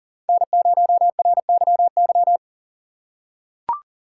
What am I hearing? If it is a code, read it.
Morse code, heard as a keyed tone, about 40 wpm, 700 Hz. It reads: N0RYY
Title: USA Call Signs - 5 Characters - 3x Character Spacing - 40wpm